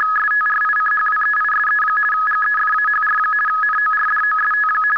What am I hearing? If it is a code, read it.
RTTY, about 50 baud. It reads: STRICHWEISE GEFAHR NORDOST
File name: article211 / rtty demo